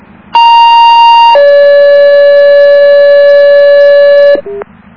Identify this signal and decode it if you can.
Two-tone page, heard as a one-second tone, then a three-second tone, about 930 then 570 Hz, with noise. It is heard from 0.4 to 4.4 s.